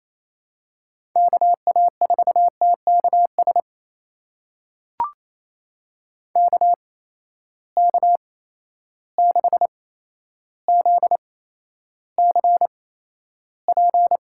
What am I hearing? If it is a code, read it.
Morse code, heard as a keyed tone, about 27 wpm, spaced as for 12 wpm, 700 Hz. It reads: KA4TKS KK6ZCP
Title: USA Call Signs - 8x Character Spacing - 28wpm